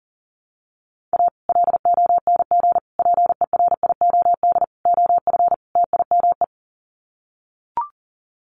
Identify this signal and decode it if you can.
Morse code, heard as a keyed tone, about 40 wpm, 700 Hz. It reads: A LONG PERIOD OF TIME
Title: Sets of 5 Words 40wpm